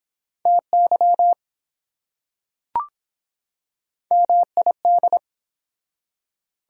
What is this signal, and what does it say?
Morse code, heard as a keyed tone, about 23 wpm, 700 Hz. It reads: TY MID